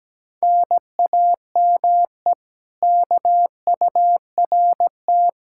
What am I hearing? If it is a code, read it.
Morse code, heard as a keyed tone, about 18 wpm, 700 Hz. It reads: NAME KURT
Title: Speed-Racing Common QSO Phrases Encore - 17wpm